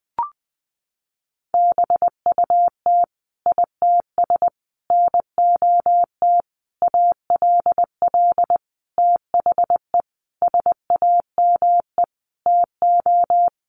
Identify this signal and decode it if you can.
Morse code, heard as a keyed tone, about 20 wpm, 700 Hz. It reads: BUT ITS NOT ALL THE SAME TO
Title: Sentences from Top 600 Words 20wpm